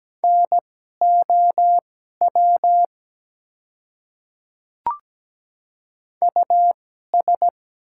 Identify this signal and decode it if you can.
Morse code, heard as a keyed tone, about 17 wpm, spaced as for 12 wpm, 700 Hz. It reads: NOW US
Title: 2x Character Spacing - Top 200 Words - 17wpm